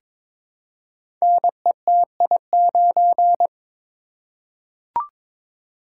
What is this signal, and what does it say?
Morse code, heard as a keyed tone, about 22 wpm, 700 Hz. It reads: NETI9
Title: Code Groups - 5 Letter:Numbers - 22wpm